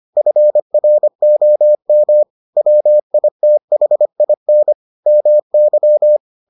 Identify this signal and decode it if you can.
Morse code, heard as a keyed tone, about 25 wpm, 600 Hz. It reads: FROM WITHIN MY